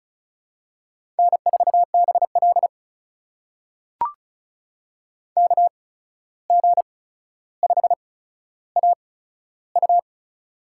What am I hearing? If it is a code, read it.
Morse code, heard as a keyed tone, about 36 wpm, spaced as for 14 wpm, 700 Hz. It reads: N4BL KG5AU
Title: USA Call Signs - 8x Character Spacing - 35wpm